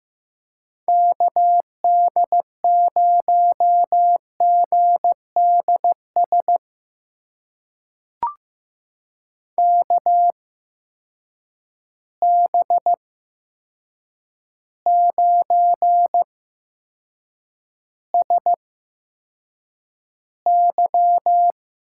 Morse code, heard as a keyed tone, about 15 wpm, spaced as for 7 wpm, 700 Hz. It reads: KD0GDS KB9SY
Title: USA Call Signs - 6 Characters - 8x Character Spacing - 15wpm